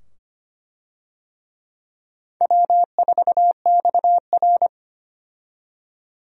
Morse code, heard as a keyed tone, about 25 wpm, 700 Hz. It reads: W4XR
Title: USA Call Signs - 4 Characters - 25wpm